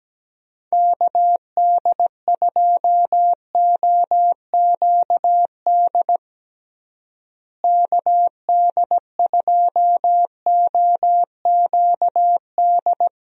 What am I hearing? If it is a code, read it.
Morse code, heard as a keyed tone, about 17 wpm, 700 Hz. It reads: KD2OQD KD2OQD
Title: USA Call Signs - 6 Characters - Repeated 3x - 17wpm